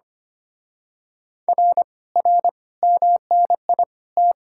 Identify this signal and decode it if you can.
Morse code, heard as a keyed tone, about 29 wpm, 700 Hz. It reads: R R MNI T
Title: Speed-Racing Common QSO Phrases - 25wpm